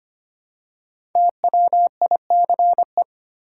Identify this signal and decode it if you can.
Morse code, heard as a keyed tone, about 25 wpm, 700 Hz. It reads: TWICE
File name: 5 Letter Words - First 3 Letters Given - 25wpm